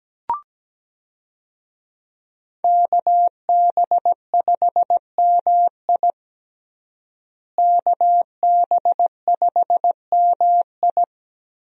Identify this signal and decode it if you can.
Morse code, heard as a keyed tone, about 17 wpm, 700 Hz. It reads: KB5MI KB5MI
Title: USA Call Signs - 5 Characters - Repeated 3x - 17wpm